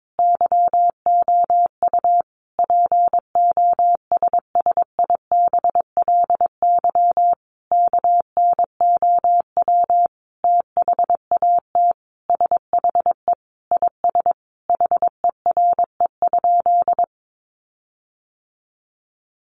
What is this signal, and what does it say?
Morse code, heard as a keyed tone, about 22 wpm, 700 Hz. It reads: YOU POSSIBLY KNOW THAT SHE IS HERE?
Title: Sentences from Top 1300 Words 22wpm